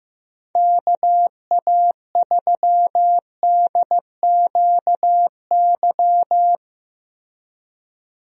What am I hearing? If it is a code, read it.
Morse code, heard as a keyed tone, about 15 wpm, 700 Hz. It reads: KA3DQY
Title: USA Call Signs - 6 Characters - Repeated 3x - 15wpm